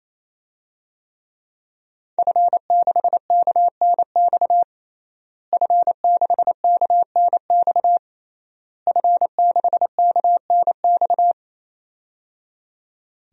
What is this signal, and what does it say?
Morse code, heard as a keyed tone, about 28 wpm, 700 Hz. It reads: F6KNX F6KNX F6KNX
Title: DXCC-100 Call Signs - 5 Characters - Repeated 3x - 28wpm